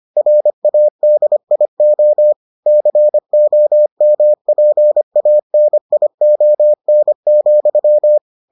Morse code, heard as a keyed tone, about 25 wpm, 600 Hz. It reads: RADIO COMPANION,